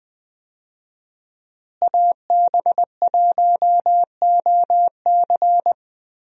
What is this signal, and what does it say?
Morse code, heard as a keyed tone, about 20 wpm, 700 Hz. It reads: AB1OC
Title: USA Call Signs - 5 Characters - Repeated 3x - 20wpm